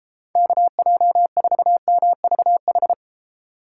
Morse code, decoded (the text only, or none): KJ4MVH